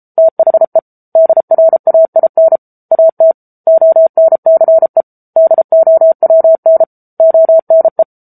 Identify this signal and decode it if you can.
Morse code, heard as a keyed tone, about 33 wpm, 650 Hz. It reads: THE DRAIN AT ONCE DOWN ONE
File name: Book Two / 07-Chapter 7 - 33 WPM